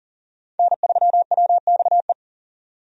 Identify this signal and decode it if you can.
Morse code, heard as a keyed tone, about 40 wpm, 700 Hz. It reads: N3WXE
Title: USA Call Signs - 5 Characters - 40wpm